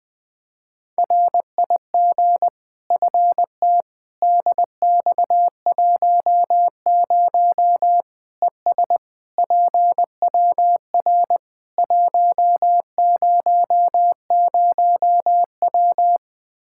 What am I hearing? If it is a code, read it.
Morse code, heard as a keyed tone, about 21 wpm, 700 Hz. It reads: RIG FT DX10 ES PWR 100W
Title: Practice QSO - Exchange 2 - Info - 20wpm